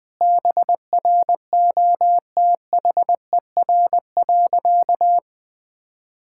Morse code, heard as a keyed tone, about 20 wpm, 700 Hz. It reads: BROTHER.
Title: Difficult Sentences with Hints - 20wpm